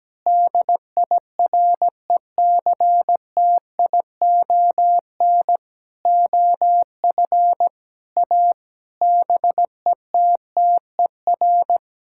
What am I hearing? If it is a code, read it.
Morse code, heard as a keyed tone, about 17 wpm, 700 Hz. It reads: DIRECTION OF A BETTER
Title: Sentences from Top 1000 Words - Review Part 5 - 17wpm